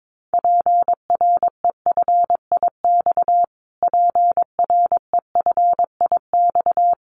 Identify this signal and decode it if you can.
Morse code, heard as a keyed tone, about 22 wpm, 700 Hz. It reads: PREFIX PREFIX
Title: Callsign Prefixes for Top 25 DXCC - Repeated 3x - 22wpm